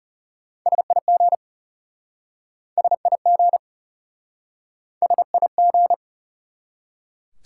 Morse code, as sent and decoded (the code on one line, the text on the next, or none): ... .. --. / ... .. --. / ... .. --.
SIG SIG SIG